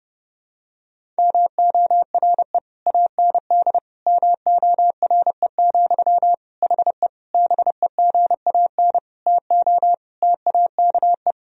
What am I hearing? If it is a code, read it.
Morse code, heard as a keyed tone, about 29 wpm, 700 Hz. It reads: MORE AND MORE, HE BEGAN TO TAKE
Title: Sentences from Top 1000 Words - Review Part 3 - 30wpm